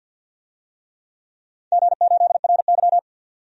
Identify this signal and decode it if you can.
Morse code, heard as a keyed tone, about 50 wpm, 700 Hz. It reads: G8RY
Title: DXCC-100 Call Signs - 4 Characters - 50wpm